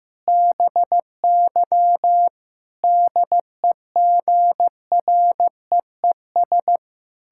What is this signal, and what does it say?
Morse code, heard as a keyed tone, about 15 wpm, 700 Hz. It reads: BY DEGREES
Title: Pangram Challenge v2 - 15wpm